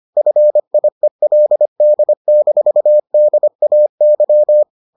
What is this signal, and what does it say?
Morse code, heard as a keyed tone, about 25 wpm, 600 Hz. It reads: FIELD-DAY